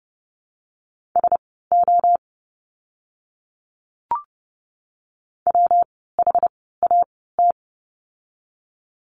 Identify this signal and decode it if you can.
Morse code, heard as a keyed tone, about 30 wpm, spaced as for 17 wpm, 700 Hz. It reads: SO WHAT
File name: course / 30:10wpm / 230 Lesson 37f - Letter x - Words 30wpm